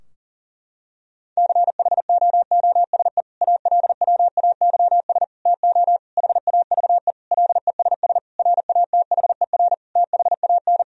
Morse code, heard as a keyed tone, about 40 wpm, 700 Hz. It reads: CHOOSE ALWAYS TO HAVE LESS RATHER THAN
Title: Sentences from Top 1000 Words - Review Part 4 - 40wpm